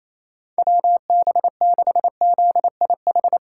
Morse code, heard as a keyed tone, about 27 wpm, 700 Hz. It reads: WB6ZIH
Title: USA Call Signs - 6 Characters - 4x Character Spacing - 28wpm